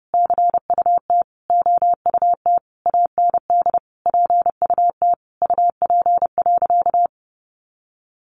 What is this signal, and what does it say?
Morse code, heard as a keyed tone, about 28 wpm, 700 Hz. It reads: CUT OUT AND PUT UP.
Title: A Sound Of Thunder 30wpm